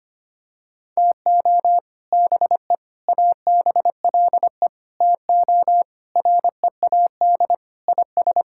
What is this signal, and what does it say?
Morse code, heard as a keyed tone, about 25 wpm, 700 Hz. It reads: TO BE ABLE TO READ IS
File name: practice / 25wpm / Sentences with 4-Word Phrases - 25wpm